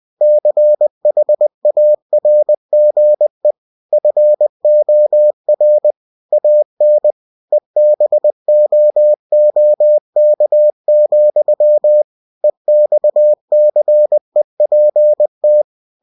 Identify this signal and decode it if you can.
Morse code, heard as a keyed tone, about 20 wpm, 600 Hz. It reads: CHARGE FOR AN EBOOK, EXCEPT
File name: cw-books / winnie-the-pooh / winnie-20wpm-page-0112